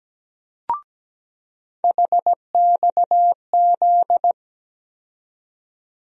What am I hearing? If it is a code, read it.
Morse code, heard as a keyed tone, about 17 wpm, 700 Hz. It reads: HXZ
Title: USA Call Sign Postfixes - 3 Letters 17wpm